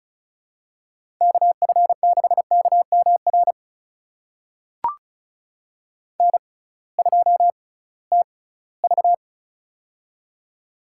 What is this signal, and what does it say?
Morse code, heard as a keyed tone, about 34 wpm, spaced as for 20 wpm, 700 Hz. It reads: KF6KMR N2TV
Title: USA Call Signs - 6x Character Spacing - 35wpm